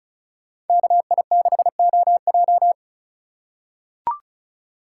Morse code, heard as a keyed tone, about 34 wpm, 700 Hz. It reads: KI6OJ
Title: USA Call Signs - 5 Characters - 3x Character Spacing - 35wpm